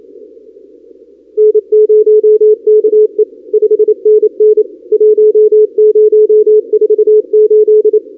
Morse code, heard as a keyed tone, about 28 wpm, 430 Hz. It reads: N0KE 5NN 1048
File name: contests / 2019-cw-wpx-kt5j-k5pi- -2330 0-32